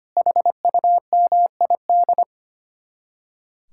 Morse code, heard as a keyed tone, about 33 wpm, 700 Hz. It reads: HUMID